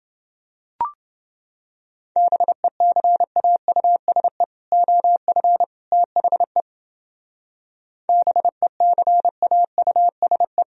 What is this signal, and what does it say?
Morse code, heard as a keyed tone, about 29 wpm, 700 Hz. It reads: BECAUSE OF THE BECAUSE